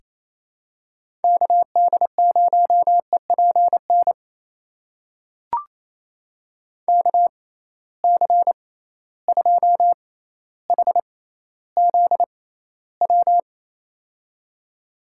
Morse code, heard as a keyed tone, about 28 wpm, spaced as for 14 wpm, 700 Hz. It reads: KD0EPN KC2HZW
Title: USA Call Signs - 6 Characters - 6x Character Spacing - 28wpm